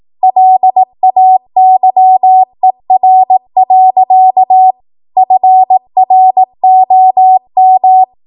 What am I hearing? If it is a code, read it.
Morse code, heard as a keyed tone, about 18 wpm, 750 Hz. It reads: LAYER. FROM
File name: Morse / 18WPM